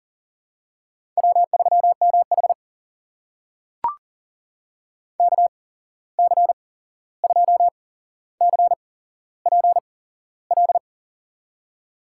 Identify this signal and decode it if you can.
Morse code, heard as a keyed tone, about 40 wpm, spaced as for 15 wpm, 700 Hz. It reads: W3MH KC2CPL
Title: USA Call Signs - 8x Character Spacing - 40wpm